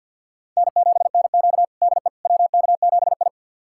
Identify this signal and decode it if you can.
Morse code, heard as a keyed tone, about 53 wpm, 700 Hz. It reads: N7NQ DE WK7I
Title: USA Callsign DE Callsign - 4 Characters - 50wpm